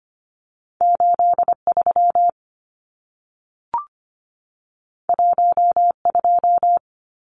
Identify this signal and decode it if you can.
Morse code, heard as a keyed tone, about 25 wpm, 700 Hz. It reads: 83 12
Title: Numbers - 2-Digit 25wpm